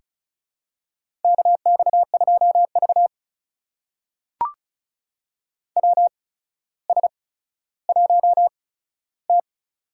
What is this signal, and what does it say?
Morse code, heard as a keyed tone, about 36 wpm, spaced as for 15 wpm, 700 Hz. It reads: KX2V WS1T
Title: USA Call Signs - 4 Characters - 8x Character Spacing - 35wpm